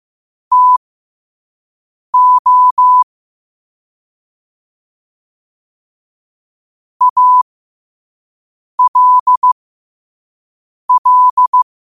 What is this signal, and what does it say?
Morse code, heard as a keyed tone, about 15 wpm, spaced as for 5 wpm, 1000 Hz. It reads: TO ALL
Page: New quotes every day in morse code at 5 Words per minute.